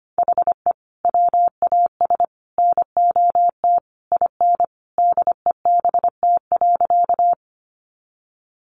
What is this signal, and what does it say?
Morse code, heard as a keyed tone, about 25 wpm, 700 Hz. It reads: HE WAS NOT IN DEBT.
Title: Sentences from Top 1300 Words 25wpm